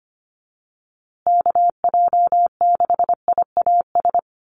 Morse code, heard as a keyed tone, about 25 wpm, 700 Hz. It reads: KJ6IAS
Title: USA Call Signs Repeated 2x 25wpm